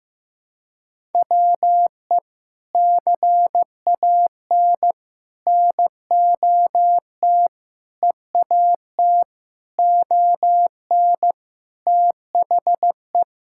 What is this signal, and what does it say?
Morse code, heard as a keyed tone, about 15 wpm, 700 Hz. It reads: WE CAN NOT EAT ON THE
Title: Sentences from Top 1000 Words - Review Part 6 - 15wpm